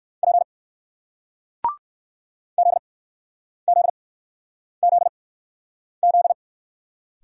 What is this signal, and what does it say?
Morse code, heard as a keyed tone, about 56 wpm, spaced as for 9 wpm, 700 Hz. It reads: P ZZZZ
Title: Speed-Racing Single Letter-Number 45wpm